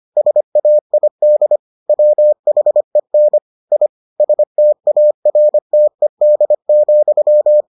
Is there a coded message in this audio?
Yes: Morse code from 0.2 to 7.6 s, heard as a keyed tone, about 25 wpm, 600 Hz. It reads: SAID WHEN I STARTED,